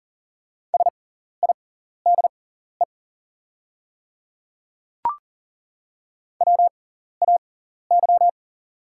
Morse code, heard as a keyed tone, about 40 wpm, spaced as for 14 wpm, 690 Hz. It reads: SIDE WAY
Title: 6x Character Spacing - Top 100 Words - 40wpm